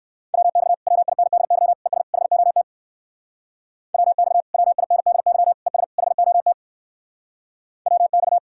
Morse code, heard as a keyed tone, about 61 wpm, 700 Hz. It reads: WX WINDY ES HOT WX WINDY ES HOT WX